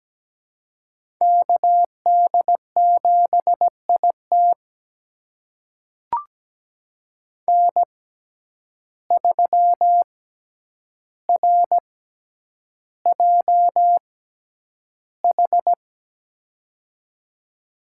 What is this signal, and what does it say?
Morse code, heard as a keyed tone, about 17 wpm, spaced as for 9 wpm, 700 Hz. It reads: KD7IT N3RJH
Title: USA Call Signs - 5 Characters - 6x Character Spacing - 17wpm